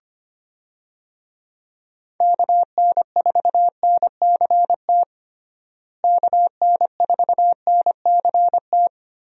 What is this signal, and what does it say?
Morse code, heard as a keyed tone, about 25 wpm, 700 Hz. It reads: KN4NCT KN4NCT
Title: USA Call Signs - 6 Characters - Repeated 2x - 25wpm